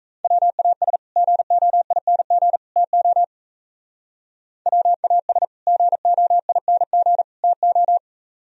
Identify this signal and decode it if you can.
Morse code, heard as a keyed tone, about 36 wpm, 700 Hz. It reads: WAS GOING TO WAS GOING TO